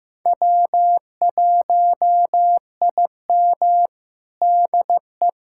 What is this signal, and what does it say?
Morse code, heard as a keyed tone, about 16 wpm, 700 Hz. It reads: W1IM DE
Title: USA Callsign DE Callsign 15wpm